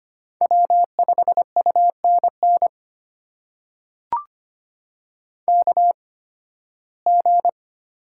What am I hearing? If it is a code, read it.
Morse code, heard as a keyed tone, about 25 wpm, spaced as for 16 wpm, 700 Hz. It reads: W5UNN KG
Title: USA Call Signs - 8x Character Spacing - 25wpm